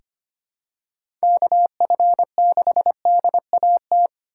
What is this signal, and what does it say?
Morse code, heard as a keyed tone, about 25 wpm, 700 Hz. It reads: KF6DAT
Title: USA Call Signs - 3x Character Spacing - 25wpm